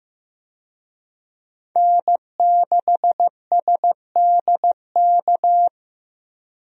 Morse code, heard as a keyed tone, about 15 wpm, 700 Hz. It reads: N6SDK
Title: USA Call Signs - 5 Characters - 15wpm